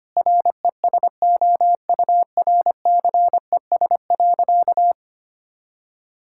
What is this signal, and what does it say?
Morse code, heard as a keyed tone, about 25 wpm, 700 Hz. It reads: RESOURCES.
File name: practice / 25wpm / Sentences with 2-Word Phrases - 25wpm